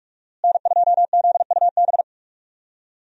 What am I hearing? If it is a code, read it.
Morse code, heard as a keyed tone, about 46 wpm, 700 Hz. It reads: N2ZUB
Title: USA Call Signs - 5 Characters - Repeated 3x - 45wpm